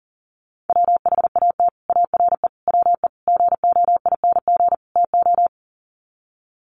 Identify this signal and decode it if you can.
Morse code, heard as a keyed tone, about 38 wpm, 700 Hz. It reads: WHAT ARE WE GOING TO
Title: Sets of 5 Words 40wpm